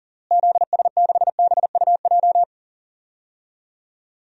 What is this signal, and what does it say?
Morse code, heard as a keyed tone, about 40 wpm, 700 Hz. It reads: ZS6BUJ
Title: DXCC-100 Call Signs - 6 Characters - Repeated 3x - 40wpm